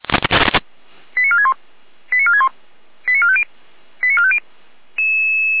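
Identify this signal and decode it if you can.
Five-tone selective call (ZVEI), heard as quick runs of five tones, tones of about 70 ms. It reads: 89461 89461 89450 89450
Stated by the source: ZVEI five-tone call received via SDR by Feuerwehr Orlamünde.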